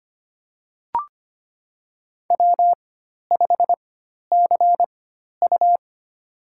Morse code, heard as a keyed tone, about 25 wpm, spaced as for 12 wpm, 700 Hz. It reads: W5CU
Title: USA Call Signs - 4x Character Spacing - 25wpm